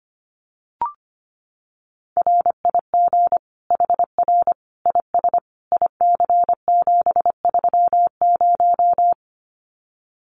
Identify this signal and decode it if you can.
Morse code, heard as a keyed tone, about 25 wpm, 700 Hz. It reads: RIG HR IS IC730
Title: Common QSO Phrases 25wpm